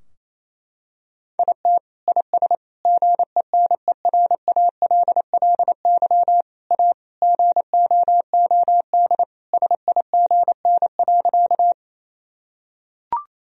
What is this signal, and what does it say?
Morse code, heard as a keyed tone, about 28 wpm, 700 Hz. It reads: IT IS GENERALLY A GOOD SIGN.
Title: Sentences from Top 1000 Words - Review Part 4 - 28wpm